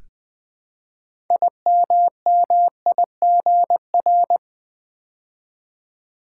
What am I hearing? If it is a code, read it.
Morse code, heard as a keyed tone, about 20 wpm, 700 Hz. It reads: IMMIGR
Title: Build A Word - 3 Letters At A Time - 9-Letter Words - 20wpm